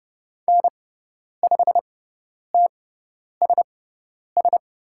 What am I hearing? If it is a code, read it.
Morse code, heard as a keyed tone, about 30 wpm, spaced as for 10 wpm, 700 Hz. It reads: N5TSS
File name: Lesson 16g - Letter c - Call Signs 30wpm